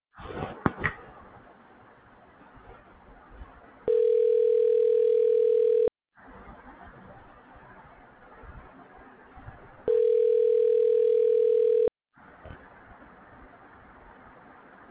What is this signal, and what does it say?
Ringback tone (North America), 440 + 480 Hz.